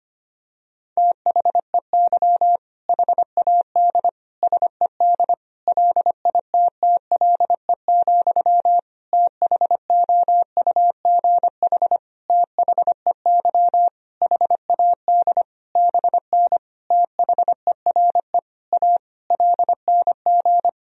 Morse code, heard as a keyed tone, about 25 wpm, 700 Hz. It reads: THEY HAD SED LITTLE, THOUGH THEY HAD BN THERE A LNG
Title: Sentences from Top 400 Words and Abbreviations 25wpm